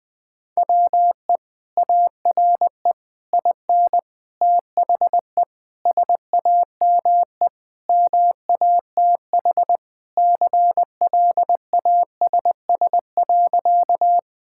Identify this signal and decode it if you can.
Morse code, heard as a keyed tone, about 20 wpm, 700 Hz. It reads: WE ARE IN THE SAME MATH CLASS.
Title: Sentences from Top 1000 Words - Review Part 6 - 20wpm